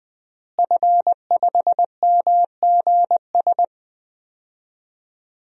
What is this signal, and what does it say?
Morse code, heard as a keyed tone, about 20 wpm, 700 Hz. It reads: F5MGS
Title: DXCC-100 Call Signs - 5 Characters - Repeated 3x - 20wpm